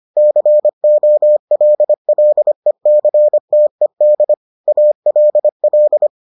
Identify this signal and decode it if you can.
Morse code, heard as a keyed tone, about 25 wpm, 600 Hz. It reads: COLLECTED ALL